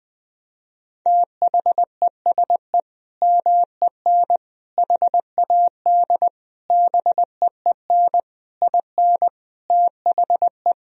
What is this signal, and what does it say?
Morse code, heard as a keyed tone, about 20 wpm, 700 Hz. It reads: THESE MEN HAD BEEN IN THE
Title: Sentences from Top 1000 Words - Review Part 1 - 20wpm